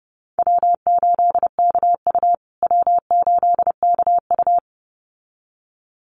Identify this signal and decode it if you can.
Morse code, heard as a keyed tone, about 30 wpm, 700 Hz. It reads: W8KU W8KU
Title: USA Call Signs Repeated 2x 30wpm